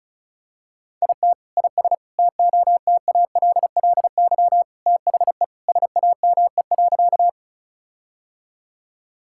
Morse code, heard as a keyed tone, about 36 wpm, 700 Hz. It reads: IT IS TOTALLY THE SAME.
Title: Sentences from Top 1600 Words 35wpm